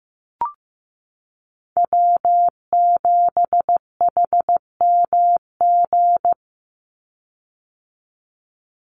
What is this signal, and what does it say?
Morse code, heard as a keyed tone, about 15 wpm, 700 Hz. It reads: W7HMG
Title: USA Call Signs 15wpm